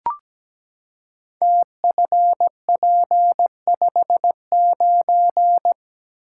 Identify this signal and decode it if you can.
Morse code, heard as a keyed tone, about 17 wpm, 700 Hz. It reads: TFP59
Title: Code Groups - 5 Letter:Numbers - 17wpm